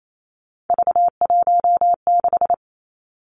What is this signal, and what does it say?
Morse code, heard as a keyed tone, about 28 wpm, 700 Hz. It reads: V16
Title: Three Challenging Letter-Number Combinations 28wpm